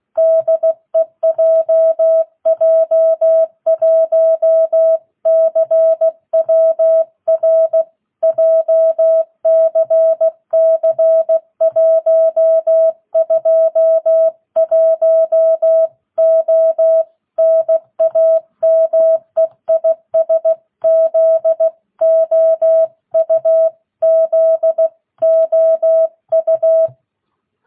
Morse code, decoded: DE JJ1CWR JCC 121O NAME IS ZOU ZOU

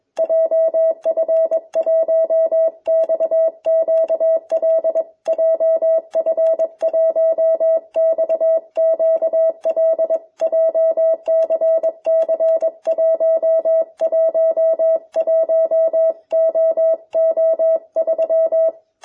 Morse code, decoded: JF1XQL JF1XQL JCC111OO3

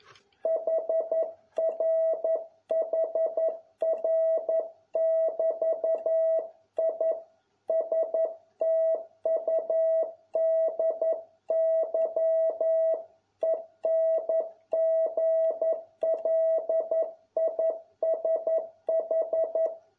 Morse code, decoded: HRHR<BT>I STUDY ENGLISH

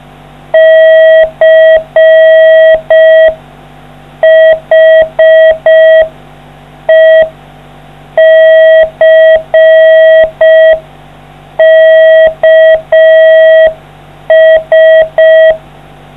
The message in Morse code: CHECKS

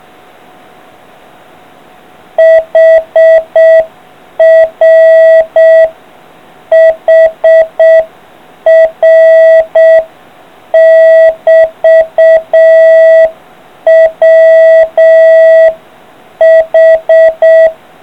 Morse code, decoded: HR HR <BT> WH

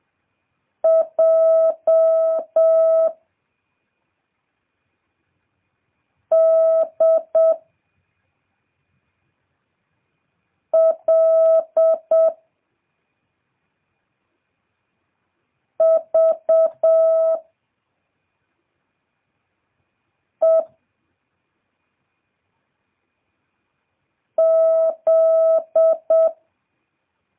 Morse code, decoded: J D L V E Z

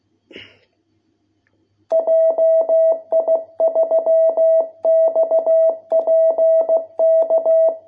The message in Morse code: JI3XPK

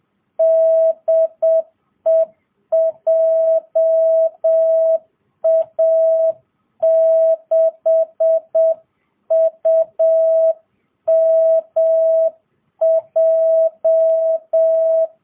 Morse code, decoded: DE JA6UMJ

